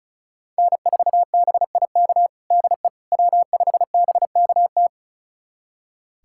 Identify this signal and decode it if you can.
Morse code, heard as a keyed tone, about 36 wpm, 700 Hz. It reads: N4BIK DE W5BKT
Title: USA Callsign DE Callsign - 5 Characters - 35wpm